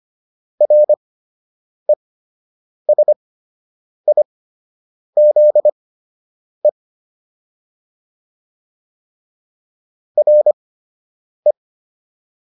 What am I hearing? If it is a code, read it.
Morse code, heard as a keyed tone, about 25 wpm, spaced as for 8 wpm, 600 Hz. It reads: RESIZE RE